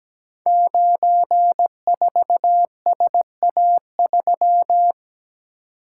Morse code, heard as a keyed tone, about 17 wpm, 700 Hz. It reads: 94SA3
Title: Code Groups - 5 Letter:Numbers - 17wpm